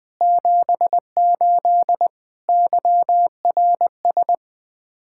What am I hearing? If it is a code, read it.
Morse code, heard as a keyed tone, about 20 wpm, 700 Hz. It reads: 78 YRS
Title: Speed-Racing Common QSO Phrases - 17wpm